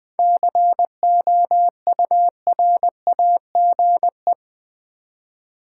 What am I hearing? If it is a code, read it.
Morse code, heard as a keyed tone, about 20 wpm, 700 Hz. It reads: COURAGE
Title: 7 Letter Words - First 3 Letters Given - 20wpm